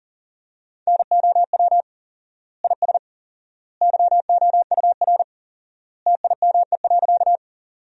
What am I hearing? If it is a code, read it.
Morse code, heard as a keyed tone, about 40 wpm, 700 Hz. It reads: NOW IS YOUR TIME.
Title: 4x Word-Spacing - Sentences from Top 100 Words - 40wpm